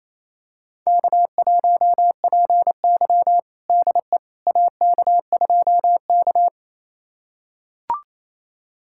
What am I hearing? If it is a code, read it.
Morse code, heard as a keyed tone, about 28 wpm, 700 Hz. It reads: K1PY DE AK2K
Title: USA Callsign DE Callsign - 4 Characters - 28wpm